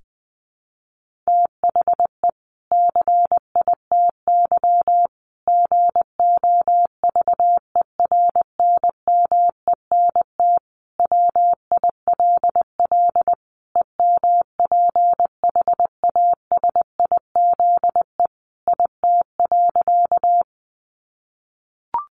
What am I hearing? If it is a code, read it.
Morse code, heard as a keyed tone, about 20 wpm, 700 Hz. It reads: THE CITY GOVERNMENT WILL EMPHASIZE IT.
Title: Sentences from Top 1500 Words 20wpm